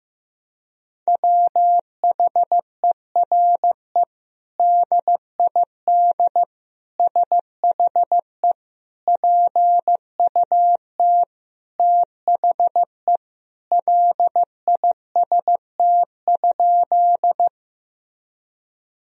Morse code, heard as a keyed tone, about 15 wpm, 700 Hz. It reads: WHERE DID SHE PUT THE LIST?
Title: Sentences from Top 1000 Words - Review Part 2 - 15wpm